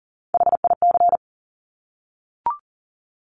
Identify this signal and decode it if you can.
Morse code, heard as a keyed tone, about 42 wpm, 700 Hz. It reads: HIC
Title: Top Three Letter Combinations in Words 40wpm